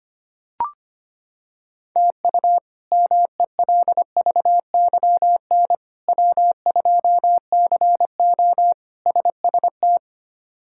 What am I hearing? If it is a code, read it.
Morse code, heard as a keyed tone, about 25 wpm, 700 Hz. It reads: TU MELVYN W2CO SST
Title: K1USN SST QSO Practice - 25wpm